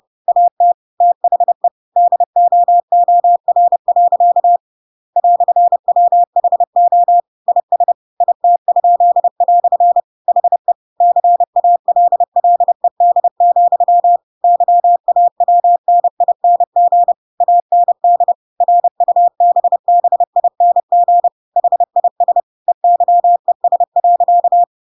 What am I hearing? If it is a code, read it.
Morse code, heard as a keyed tone, about 30 wpm, 700 Hz. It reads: AT THE DOOR. "WHO IS IT?" HE CALLED, YAWNING AND RUBBING HIS EYES.
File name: Pinocchio / Pinocchio ch6 30 30 ik1hge